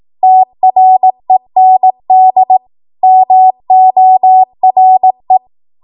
Morse code, heard as a keyed tone, about 18 wpm, 750 Hz. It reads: TREND MORE
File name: Morse / 18WPM